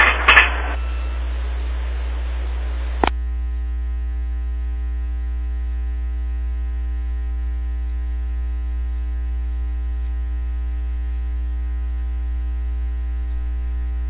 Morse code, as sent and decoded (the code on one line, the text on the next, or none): none